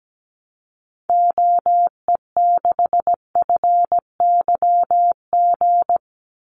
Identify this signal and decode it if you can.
Morse code, heard as a keyed tone, about 17 wpm, 700 Hz. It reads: OE6FYG
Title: Call Signs Repeated 3x - 17wpm